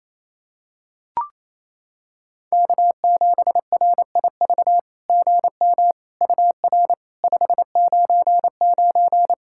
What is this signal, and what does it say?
Morse code, heard as a keyed tone, about 28 wpm, 700 Hz. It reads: K7RIV GM UR 599
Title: SOTA QSO Practice 28wpm